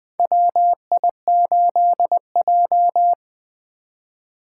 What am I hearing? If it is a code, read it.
Morse code, heard as a keyed tone, about 20 wpm, 700 Hz. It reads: WI8J
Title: 2023 SOTA Chaser Call Signs - 20wpm